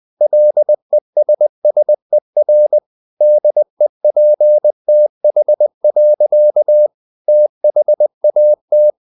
Morse code, heard as a keyed tone, about 20 wpm, 600 Hz. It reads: LESSER DEPTH. THAT
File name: the old man and the sea / the old man and the sea part 1 20wpm